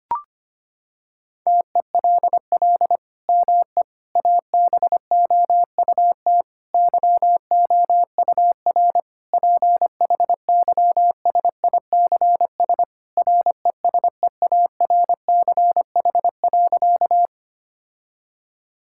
Morse code, heard as a keyed tone, about 25 wpm, 700 Hz. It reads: TELL ME ABOUT YOUR PHYSICS RESEARCH.
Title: Sentences from Top 1000 Words - Review Part 5 - 25wpm